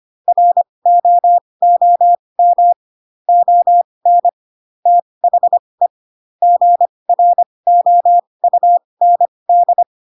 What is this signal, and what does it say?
Morse code, heard as a keyed tone, about 25 wpm, spaced as for 20 wpm, 700 Hz. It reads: ROOM ON THE GROUND
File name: Pinocchio / Pinocchio ch3 15 25 ik1hge